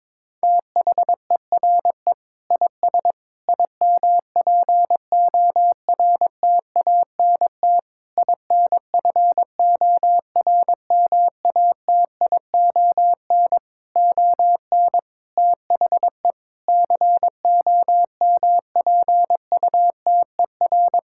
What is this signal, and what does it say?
Morse code, heard as a keyed tone, about 22 wpm, 700 Hz. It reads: THERE IS IMPORTANT INFORMATION ON THE COMPUTER
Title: Sentences from Top 1000 Words - Review Part 6 - 22wpm